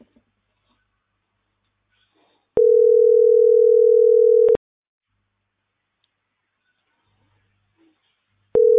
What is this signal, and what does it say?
Ringback tone (North America), 440 + 480 Hz.